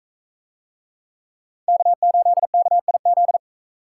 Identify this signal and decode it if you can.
Morse code, heard as a keyed tone, about 42 wpm, 700 Hz. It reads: K8KIZ